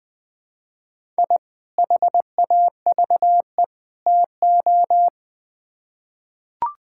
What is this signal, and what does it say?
Morse code, heard as a keyed tone, about 20 wpm, 700 Hz. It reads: I HAVE TO
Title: Build A Phrase - 3-Word Phrases - 20wpm